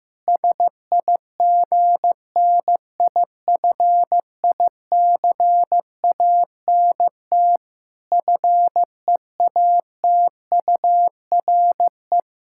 Morse code, heard as a keyed tone, about 16 wpm, 700 Hz. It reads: SIGNIFICANT FEATURE
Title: Sentences from Top 1000 Words - Review Part 6 - 15wpm